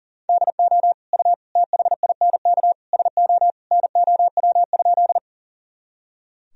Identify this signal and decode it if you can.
Morse code, heard as a keyed tone, about 40 wpm, 700 Hz. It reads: DO U THINK SO NOW?
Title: Sentences from Top 200 Words and Abbreviations 40wpm